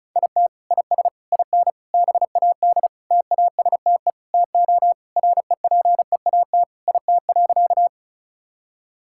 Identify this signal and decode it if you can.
Morse code, heard as a keyed tone, about 36 wpm, 700 Hz. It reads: IT IS IN BAD TASTE TO REPEAT IT.
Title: Sentences from Top 1600 Words 35wpm